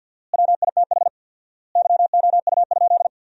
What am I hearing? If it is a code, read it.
Morse code, heard as a keyed tone, about 50 wpm, 700 Hz. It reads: WITH YOU?
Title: 4x Word-Spacing - Sentences from Top 100 Words - 50wpm